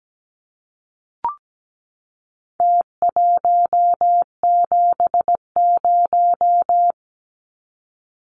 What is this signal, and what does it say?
Morse code, heard as a keyed tone, about 17 wpm, 700 Hz. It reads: T170
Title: Numbers - Contest Style - 4-Digit 17wpm